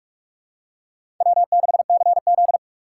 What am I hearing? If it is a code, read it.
Morse code, heard as a keyed tone, about 47 wpm, 700 Hz. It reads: W6CZ
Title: USA Call Signs - 6x Character Spacing - 45wpm